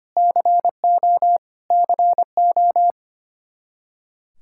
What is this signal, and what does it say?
Morse code, heard as a keyed tone, about 25 wpm, 700 Hz. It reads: CO CO
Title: Practice QSO - Exchange 1 - Beginning - 25wpm